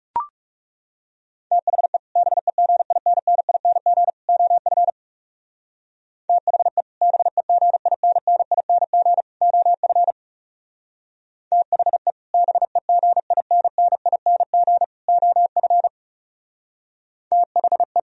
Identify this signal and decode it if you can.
Morse code, heard as a keyed tone, about 38 wpm, 700 Hz. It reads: THE BEGINNING OF THE BEGINNING OF THE BEGINNING OF THE